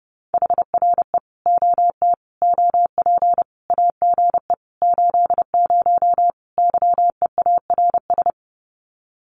Morse code, heard as a keyed tone, about 28 wpm, 700 Hz. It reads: HRE OT OP AGE 80 YEARS
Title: Common QSO Phrases 30wpm